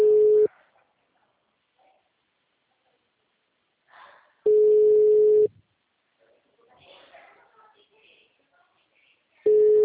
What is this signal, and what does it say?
Ringback tone (Europe, 425 Hz), with speech.